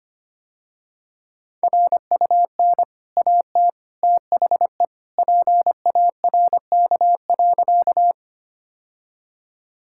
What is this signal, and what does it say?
Morse code, heard as a keyed tone, about 25 wpm, 690 Hz. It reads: RUN AT THE PARK.
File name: Easy Sentences with Hints - 25wpm